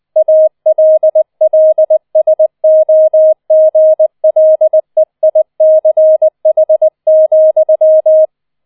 Morse code, heard as a keyed tone, about 20 wpm, 630 Hz. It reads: ALLSOGLEICH,